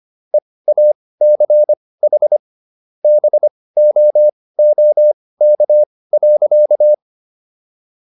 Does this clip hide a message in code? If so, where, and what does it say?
Yes: Morse code from 0.3 to 6.9 s, heard as a keyed tone, about 25 wpm, spaced as for 18 wpm, 600 Hz. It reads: EACH BOOK.